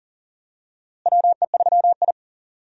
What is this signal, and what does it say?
Morse code, heard as a keyed tone, about 40 wpm, 700 Hz. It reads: WE3I
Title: USA Call Signs - 4 Characters - Repeated 3x - 40wpm